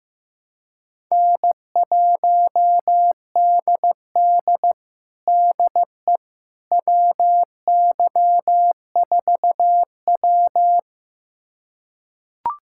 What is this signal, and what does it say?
Morse code, heard as a keyed tone, about 15 wpm, 700 Hz. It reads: N1DD DE WY4W
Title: USA Callsign DE Callsign - 4 Characters - 15wpm